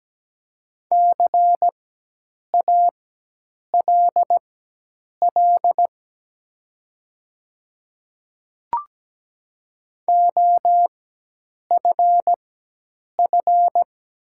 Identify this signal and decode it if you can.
Morse code, heard as a keyed tone, about 17 wpm, spaced as for 8 wpm, 700 Hz. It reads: CALL OFF
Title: 4x Character Spacing - Top 200 Words - 17wpm